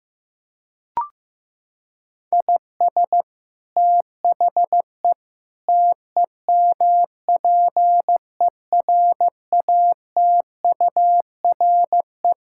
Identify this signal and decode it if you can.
Morse code, heard as a keyed tone, about 15 wpm, 700 Hz. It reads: IS THE TEMPERATURE
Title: Sentences from Top 1000 Words - Review Part 6 - 15wpm